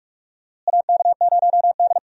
Morse code, heard as a keyed tone, about 45 wpm, 700 Hz. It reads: AK0D
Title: USA Call Signs - 4 Characters - Repeated 3x - 45wpm